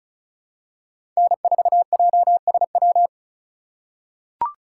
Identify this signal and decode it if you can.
Morse code, heard as a keyed tone, about 34 wpm, 700 Hz. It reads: N4JSW
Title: USA Call Signs - 2x Character Spacing - 35wpm